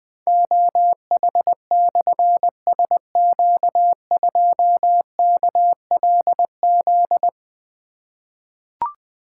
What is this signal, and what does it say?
Morse code, heard as a keyed tone, about 20 wpm, 700 Hz. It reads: OH/SQ2KLZ
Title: Difficult Call Signs 20wpm